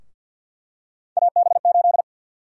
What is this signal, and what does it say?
Morse code, heard as a keyed tone, about 51 wpm, 700 Hz. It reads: AB8
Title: USA Call Sign Prefixes - 50wpm